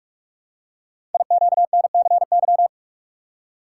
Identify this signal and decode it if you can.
Morse code, heard as a keyed tone, about 44 wpm, 700 Hz. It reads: IQNCY
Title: Code Groups - 5 Letters - 45wpm